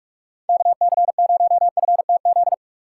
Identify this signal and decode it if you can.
Morse code, heard as a keyed tone, about 44 wpm, 700 Hz. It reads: KC0FTZ
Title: USA Call Signs - 6 Characters - 4x Character Spacing - 45wpm